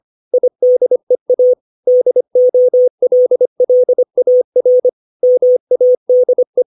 Morse code, heard as a keyed tone, about 25 wpm, 500 Hz. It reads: IDEA DOLLAR MADE
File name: 632 words / 632 Words 25WPM